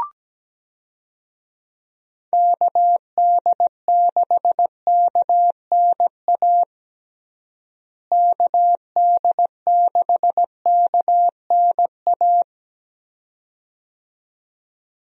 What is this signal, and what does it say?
Morse code, heard as a keyed tone, about 17 wpm, 700 Hz. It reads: KD6KNA KD6KNA
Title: USA Call Signs - 6 Characters - Repeated 2x - 17wpm